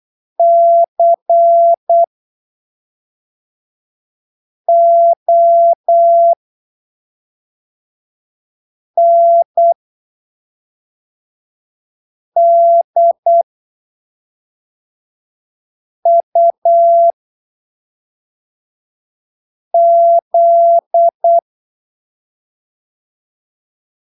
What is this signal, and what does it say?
Morse code, heard as a keyed tone, about 8 wpm, spaced as for 3 wpm, 680 Hz. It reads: CONDUZ
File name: arquivos / texto / texto 05 08ppm